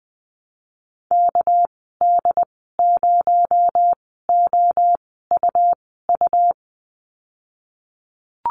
Morse code, heard as a keyed tone, about 20 wpm, spaced as for 15 wpm, 700 Hz. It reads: KD0OUU
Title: 238 Lesson 38g - Letter q - Call Signs 20wpm